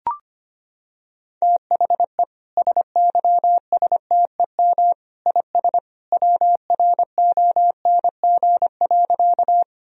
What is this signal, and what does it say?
Morse code, heard as a keyed tone, about 25 wpm, 700 Hz. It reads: THE SYSTEM IS WRONG.
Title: Sentences from Top 1000 Words - Review Part 3 - 25wpm